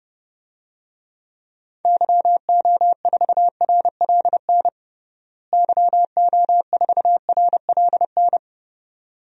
Morse code, heard as a keyed tone, about 29 wpm, 700 Hz. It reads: YO4RLN YO4RLN
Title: DXCC-100 Call Signs - 6 Characters - Repeated 2x - 30wpm